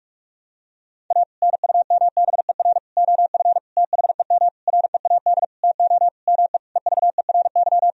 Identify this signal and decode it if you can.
Morse code, heard as a keyed tone, about 44 wpm, 700 Hz. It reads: A NUMBER OF THEM READ TO ME EVERY
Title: Sentences from Top 1000 Words - Review Part 1 - 45wpm